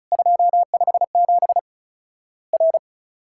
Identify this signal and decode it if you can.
Morse code, heard as a keyed tone, about 36 wpm, 700 Hz. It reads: 257 R
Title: Worked All Europe DX Contest - QTC Exchange - 35wpm